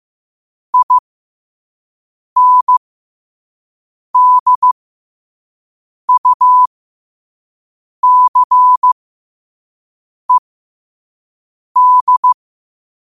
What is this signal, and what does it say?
Morse code, heard as a keyed tone, about 15 wpm, spaced as for 5 wpm, 1000 Hz. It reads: INDUCED
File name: qotdfiles / 5wpm